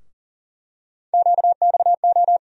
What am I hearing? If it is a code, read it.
Morse code, heard as a keyed tone, about 40 wpm, 700 Hz. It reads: QXO
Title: USA Call Sign Postfixes - 3 Letters 40wpm